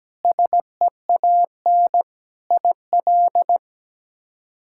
Morse code, heard as a keyed tone, about 17 wpm, 700 Hz. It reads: SEAN IL
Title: K1USN SST QSO Practice - 17wpm